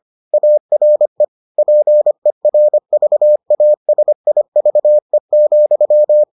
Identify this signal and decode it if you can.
Morse code, heard as a keyed tone, about 25 wpm, 600 Hz. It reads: ARE PERVASIVE,